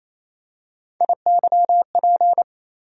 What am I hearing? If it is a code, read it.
Morse code, heard as a keyed tone, about 28 wpm, 700 Hz. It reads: IYP
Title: USA Call Sign Postfixes - 3 Letters 28wpm